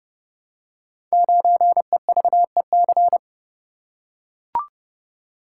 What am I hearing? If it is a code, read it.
Morse code, heard as a keyed tone, about 31 wpm, 700 Hz. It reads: 9EVEC
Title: Code Groups - 5 Letter:Numbers - 30wpm